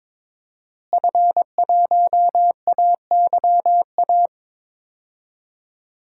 Morse code, heard as a keyed tone, about 22 wpm, 700 Hz. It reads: F1AYA
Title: DXCC-100 Call Signs - 5 Characters - Repeated 2x - 22wpm